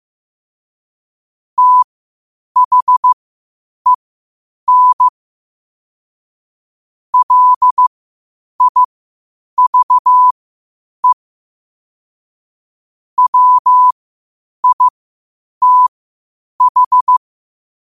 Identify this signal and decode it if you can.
Morse code, heard as a keyed tone, about 15 wpm, spaced as for 8 wpm, 1000 Hz. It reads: THEN LIVE WITH
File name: qotdfiles / 8wpm